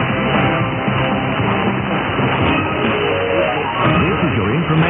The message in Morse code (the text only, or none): none